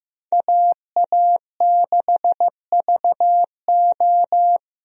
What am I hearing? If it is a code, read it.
Morse code, heard as a keyed tone, about 15 wpm, 700 Hz. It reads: AA6VO
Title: USA Call Signs - 5 Characters - Repeated 2x - 15wpm